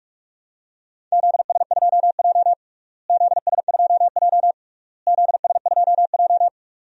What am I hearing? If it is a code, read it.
Morse code, heard as a keyed tone, about 46 wpm, 700 Hz. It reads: ZS2J ZS2J ZS2J
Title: DXCC-100 Call Signs - 4 Characters - Repeated 3x - 45wpm